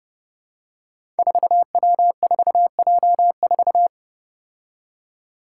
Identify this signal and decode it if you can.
Morse code, heard as a keyed tone, about 28 wpm, 700 Hz. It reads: 4W4J4
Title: Five Challenging Letter-Number Combinations 30wpm